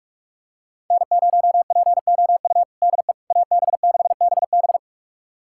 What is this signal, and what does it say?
Morse code, heard as a keyed tone, about 42 wpm, 700 Hz. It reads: N0POU DE AB6BB
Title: USA Callsign DE Callsign - 5 Characters - 45wpm